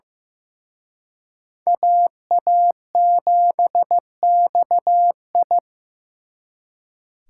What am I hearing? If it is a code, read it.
Morse code, heard as a keyed tone, about 18 wpm, 700 Hz. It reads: AA7XI
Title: Speed Racing - USA 2x2 Call Signs 15wpm